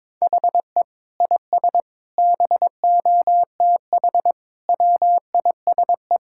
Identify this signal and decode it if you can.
Morse code, heard as a keyed tone, about 22 wpm, 700 Hz. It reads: HE IS BOTH WISE
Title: Sentences from Top 1600 Words 22wpm